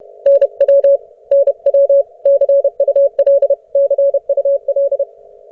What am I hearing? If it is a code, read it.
Morse code, heard as a keyed tone, about 32 wpm, 590 Hz. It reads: NW NW CUL CUL